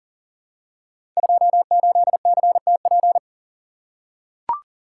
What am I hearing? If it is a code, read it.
Morse code, heard as a keyed tone, about 40 wpm, 700 Hz. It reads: 28CTP
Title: Code Groups - 5 Letter:Numbers - 40wpm